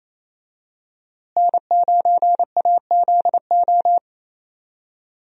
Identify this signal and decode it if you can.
Morse code, heard as a keyed tone, about 28 wpm, 700 Hz. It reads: N9AZO
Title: USA Call Signs - 5 Characters - Repeated 2x - 28wpm